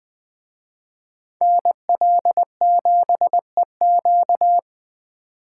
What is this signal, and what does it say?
Morse code, heard as a keyed tone, about 20 wpm, 700 Hz. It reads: NL7EQ
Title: Speed Racing - USA 2x2 Call Signs 17wpm